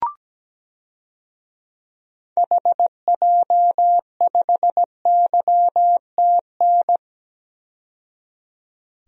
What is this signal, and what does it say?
Morse code, heard as a keyed tone, about 17 wpm, 700 Hz. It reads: HJ5YTN
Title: DXCC-100 Call Signs - 6 Characters - 17wpm